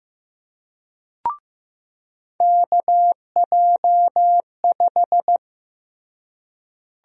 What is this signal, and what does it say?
Morse code, heard as a keyed tone, about 16 wpm, 700 Hz. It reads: KJ5
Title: USA Call Sign Prefixes - Repeated 2x - 15wpm